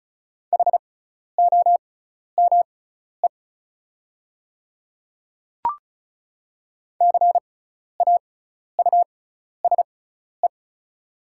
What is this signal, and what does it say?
Morse code, heard as a keyed tone, about 34 wpm, spaced as for 12 wpm, 690 Hz. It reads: HOME CAUSE
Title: 6x Character Spacing - Top 300 Words - 35wpm